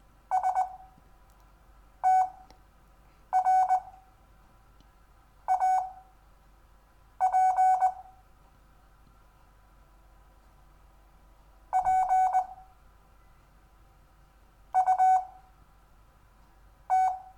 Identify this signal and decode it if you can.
Morse code, heard as a keyed tone, about 20 wpm, spaced as for 5 wpm, 750 Hz. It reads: STRAP PUT
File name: cw / Lesson4 REVIEW